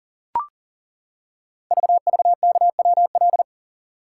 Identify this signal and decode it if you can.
Morse code, heard as a keyed tone, about 40 wpm, 700 Hz. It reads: VVKWL
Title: Code Groups - 5 Letters - 40wpm